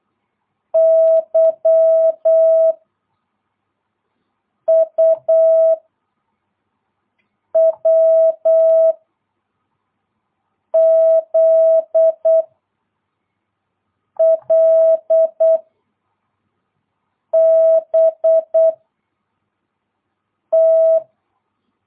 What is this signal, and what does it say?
Morse code, heard as a keyed tone, about 8 wpm, 650 Hz. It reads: Y U W Z L B T
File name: jh8cbh / 2022cw